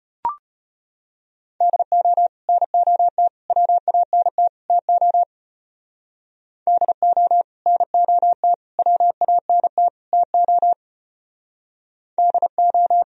Morse code, heard as a keyed tone, about 31 wpm, 700 Hz. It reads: DO NOT WANT TO DO NOT WANT TO DO